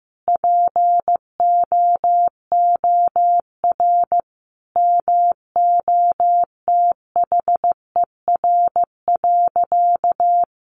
Morse code, heard as a keyed tone, about 15 wpm, 700 Hz. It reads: POOR MOTHER.
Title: Sentences from Top 1000 Words 15wpm